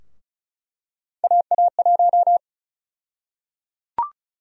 Morse code, heard as a keyed tone, about 34 wpm, 700 Hz. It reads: AA1
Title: USA Call Sign Prefixes - 35wpm